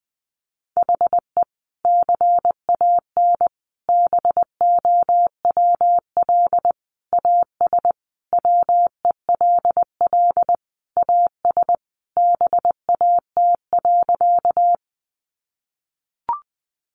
Morse code, heard as a keyed tone, about 20 wpm, 700 Hz. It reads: HE CAN BOWL AS WELL AS BAT.
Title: Sentences from Top 1500 Words 20wpm